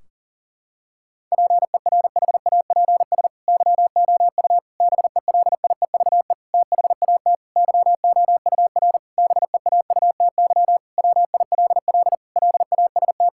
Morse code, heard as a keyed tone, about 40 wpm, 700 Hz. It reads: PERHAPS YOU BELIEVE THAT YOUR BEAUTY WILL LAST
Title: Sentences from Top 1600 Words 40wpm